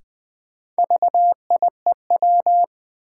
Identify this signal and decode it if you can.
Morse code, heard as a keyed tone, about 20 wpm, 700 Hz. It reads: VIEW
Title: Sets of 5 Words - New Word Drill - 20wpm